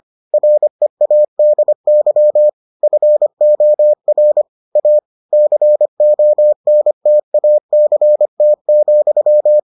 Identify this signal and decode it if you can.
Morse code, heard as a keyed tone, about 25 wpm, 600 Hz. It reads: READY FOR A CONTACT,